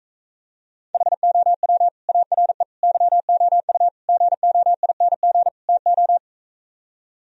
Morse code, heard as a keyed tone, about 45 wpm, 700 Hz. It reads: HOW ARE YOU GOING TO